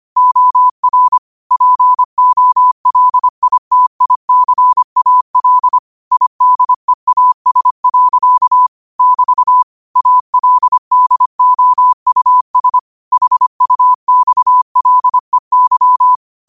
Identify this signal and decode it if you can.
Morse code, heard as a keyed tone, about 25 wpm, 1000 Hz. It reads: OR POLITICAL IDEAS. <BT> ALDOUS HUXLEY
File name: qotdfiles / 25wpm